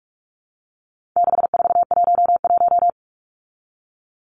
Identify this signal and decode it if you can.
Morse code, heard as a keyed tone, about 44 wpm, 700 Hz. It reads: 6411
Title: Numbers - 4-Digit 45wpm